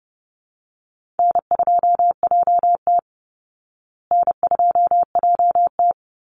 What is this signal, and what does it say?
Morse code, heard as a keyed tone, about 28 wpm, 700 Hz. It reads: N2JT N2JT
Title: Easy Call Signs Repeated 3x - 30wpm